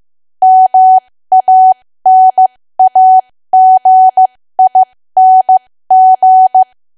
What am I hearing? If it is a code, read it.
Morse code, heard as a keyed tone, about 15 wpm, spaced as for 13 wpm, 750 Hz. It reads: MANAGING
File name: cwa / W1AW-Level2 / 13WPM